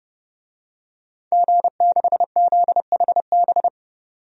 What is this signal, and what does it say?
Morse code, heard as a keyed tone, about 32 wpm, 700 Hz. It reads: G6ZHB
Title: DXCC-100 Call Signs - 5 Characters - 30wpm